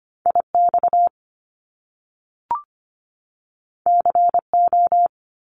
Morse code, heard as a keyed tone, about 25 wpm, 700 Hz. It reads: IX CO
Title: Top Two Letter Combinations in Words 25wpm